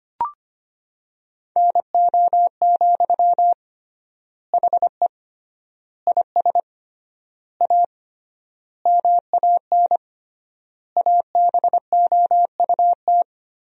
Morse code, heard as a keyed tone, about 25 wpm, 700 Hz. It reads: NO, HE IS A MAN ABOUT
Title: 3x Word-Spacing - Sentences from Top 400 Words - 25wpm